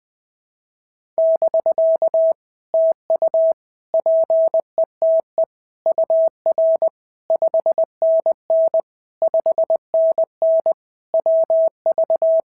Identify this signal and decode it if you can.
Morse code, heard as a keyed tone, about 20 wpm, 650 Hz. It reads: <BK> TU PETE UR 5NN 5NN WV
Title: POTA QSO Practice - 20wpm